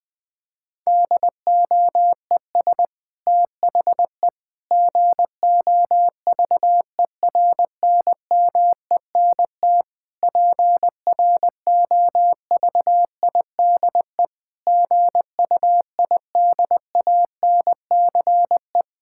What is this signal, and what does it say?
Morse code, heard as a keyed tone, about 20 wpm, 700 Hz. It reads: DOES THE GOVERNMENT PROVIDE GUIDANCE
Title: Sentences from Top 1600 Words 20wpm